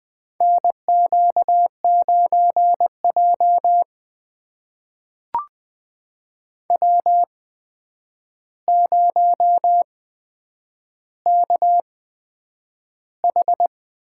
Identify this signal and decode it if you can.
Morse code, heard as a keyed tone, about 20 wpm, spaced as for 9 wpm, 700 Hz. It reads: NQ9J W0KH
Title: USA Call Signs - 4 Characters - 8x Character Spacing - 20wpm